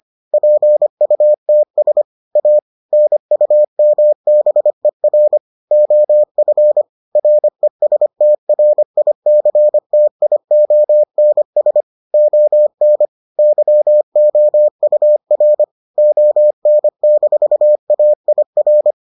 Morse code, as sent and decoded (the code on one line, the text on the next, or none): .--. ..- - ... / .- / -. ..- -- -... .-. / --- ..-. / .-. ... - .-. .. -.-. - .. --- -. ... / --- -. / -.-- --- ..- .-. / --- -. -....- .- .. .-.
PUTS A NUMBER OF RESTRICTIONS ON YOUR ON-AIR